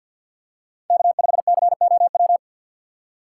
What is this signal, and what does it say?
Morse code, heard as a keyed tone, about 52 wpm, 700 Hz. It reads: K5COW
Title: USA Call Signs - 5 Characters - 2x Character Spacing - 50wpm